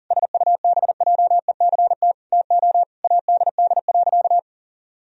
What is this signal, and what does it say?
Morse code, heard as a keyed tone, about 40 wpm, 700 Hz. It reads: SUBJECT TO ADD.
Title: Sentences from Top 1000 Words - Review Part 1 - 40wpm